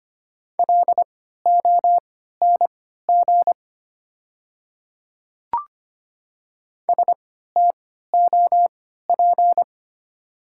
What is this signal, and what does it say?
Morse code, heard as a keyed tone, about 25 wpm, spaced as for 14 wpm, 700 Hz. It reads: LONG STOP
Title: 3x Character Spacing - Top 300 Words - 25wpm